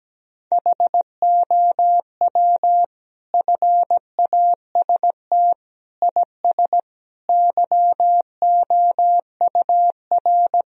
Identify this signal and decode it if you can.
Morse code, heard as a keyed tone, about 17 wpm, 700 Hz. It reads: HOW FAST IS YOUR
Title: Sentences from Top 1000 Words - Review Part 4 - 17wpm